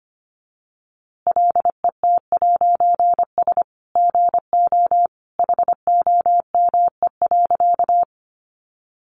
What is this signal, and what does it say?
Morse code, heard as a keyed tone, about 25 wpm, 700 Hz. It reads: LET'S GO HOME.
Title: A Sound Of Thunder 25wpm